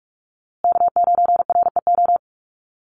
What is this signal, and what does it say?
Morse code, heard as a keyed tone, about 44 wpm, 700 Hz. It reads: K9REO
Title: USA Call Signs Repeated 3x 45wpm